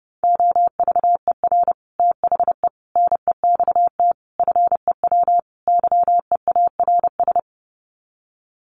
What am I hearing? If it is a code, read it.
Morse code, heard as a keyed tone, about 30 wpm, 700 Hz. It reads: OVER THE NEXT FEW YEARS
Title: Sets of 5 Words Encore 30wpm